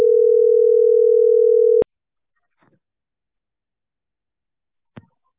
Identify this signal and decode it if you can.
Ringback tone (North America), 440 + 480 Hz.